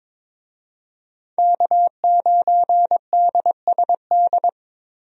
Morse code, heard as a keyed tone, about 22 wpm, 700 Hz. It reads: K9DSD
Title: USA Call Signs - 5 Characters - Repeated 3x - 22wpm